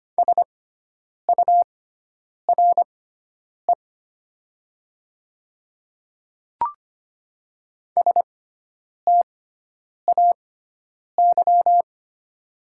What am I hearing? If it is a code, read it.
Morse code, heard as a keyed tone, about 25 wpm, spaced as for 9 wpm, 700 Hz. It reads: SURE STAY
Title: 6x Character Spacing - Top 500 Words - 25wpm